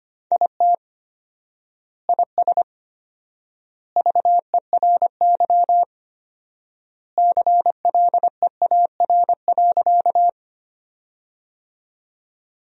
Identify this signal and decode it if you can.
Morse code, heard as a keyed tone, about 25 wpm, 700 Hz. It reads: IT IS VERY CLEAR.
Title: 4x Word-Spacing - Sentences from Top 500 Words - 25wpm